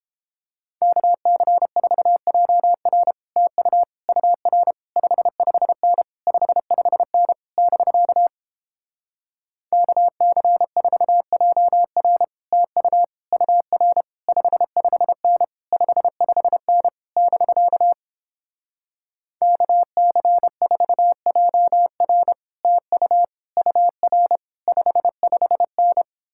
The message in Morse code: KC4JR TU UR 55N 55N <BK> KC4JR TU UR 55N 55N <BK> KC4JR TU UR 55N

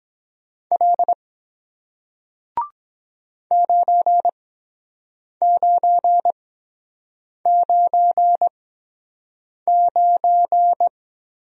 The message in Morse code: L 9999

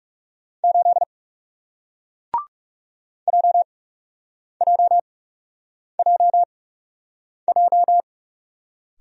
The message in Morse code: 8 JJJJ

